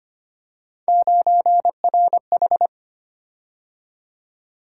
Morse code, heard as a keyed tone, about 25 wpm, 700 Hz. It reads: 9RH
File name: Three Letter-Number Combinations - 25wpm